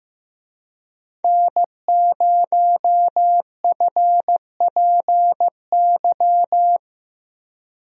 Morse code, heard as a keyed tone, about 15 wpm, 700 Hz. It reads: N0FPY
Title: USA Call Signs - 5 Characters - 15wpm